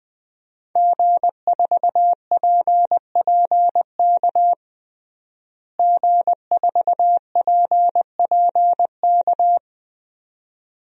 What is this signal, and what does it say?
Morse code, heard as a keyed tone, about 20 wpm, 700 Hz. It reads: G4PPK G4PPK
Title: DXCC-100 Call Signs - 5 Characters - Repeated 2x - 20wpm